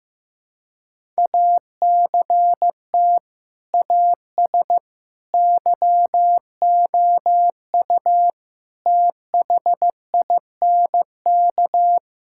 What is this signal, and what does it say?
Morse code, heard as a keyed tone, about 15 wpm, 700 Hz. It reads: ACT AS YOU THINK
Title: Sentences from Top 1000 Words - Review Part 5 - 15wpm